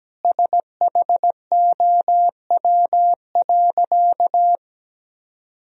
Morse code, heard as a keyed tone, about 17 wpm, 700 Hz. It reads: SHOW.
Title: Sentences from Top 1000 Words - Review Part 2 - 17wpm